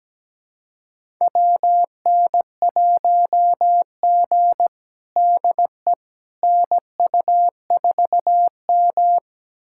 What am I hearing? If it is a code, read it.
Morse code, heard as a keyed tone, about 17 wpm, 700 Hz. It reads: WN1G DE NU4M
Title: USA Callsign DE Callsign - 4 Characters - 17wpm